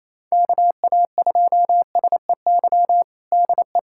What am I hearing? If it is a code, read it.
Morse code, heard as a keyed tone, about 28 wpm, 700 Hz. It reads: KA2SEY DE
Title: USA Callsign DE Callsign 28wpm